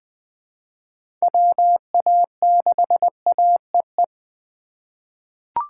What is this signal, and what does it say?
Morse code, heard as a keyed tone, about 20 wpm, 700 Hz. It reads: WA6AEE
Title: K1USN SST QSO Practice - 20wpm